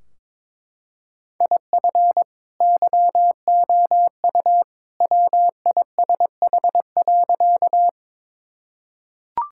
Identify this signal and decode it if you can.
Morse code, heard as a keyed tone, about 22 wpm, 700 Hz. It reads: IF YOU WISH.
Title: Sentences from Top 1600 Words 22wpm